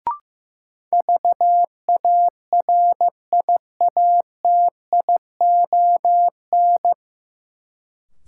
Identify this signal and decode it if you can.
Morse code, heard as a keyed tone, about 15 wpm, 700 Hz. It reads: VARIATION
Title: Sentences from Top 1600 Words - New Word Drill - 15wpm